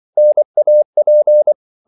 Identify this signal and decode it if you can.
Morse code, heard as a keyed tone, about 24 wpm, 600 Hz. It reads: NAP
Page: Korte danske ord 24wpm